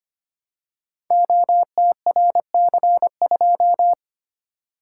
Morse code, heard as a keyed tone, about 25 wpm, 700 Hz. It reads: OTRC2
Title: Code Groups - 5 Letter:Numbers - 25wpm